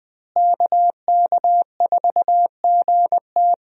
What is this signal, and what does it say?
Morse code, heard as a keyed tone, about 20 wpm, 700 Hz. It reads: KK4GT
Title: USA Call Signs - 5 Characters - Repeated 2x - 20wpm